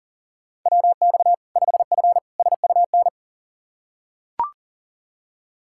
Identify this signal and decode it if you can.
Morse code, heard as a keyed tone, about 40 wpm, 700 Hz. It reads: WX 5F SUN
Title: Practice QSO - Exchange 2 - Info - 40wpm